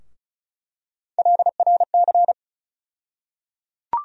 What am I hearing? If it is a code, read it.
Morse code, heard as a keyed tone, about 36 wpm, 700 Hz. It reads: LRC
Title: USA Call Sign Postfixes - 3 Letters 35wpm